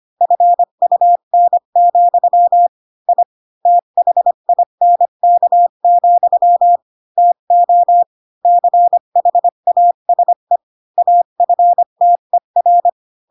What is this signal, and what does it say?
Morse code, heard as a keyed tone, about 25 wpm, 700 Hz. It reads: FUN, I THINK, TO CHASE AFTER
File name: Pinocchio / Pinocchio ch4 20 25 ik1hge